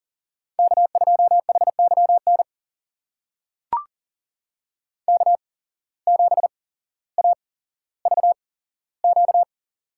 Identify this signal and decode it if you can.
Morse code, heard as a keyed tone, about 40 wpm, spaced as for 17 wpm, 700 Hz. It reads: K2HYN K7AVQ
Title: USA Call Signs - 5 Characters - 8x Character Spacing - 40wpm